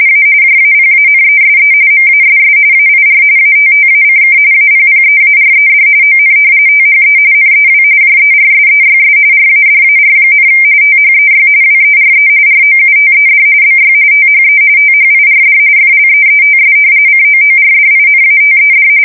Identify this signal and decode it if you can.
RTTY, about 45.45 baud. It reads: A JAPANESE MASS AIR ATTACK APPEARED ON THE NEW JERSEYS RADAR, THEY WERE ROUTED BY THE THIRD FLEETS COORDINATED
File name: fsk170 / battleshipnj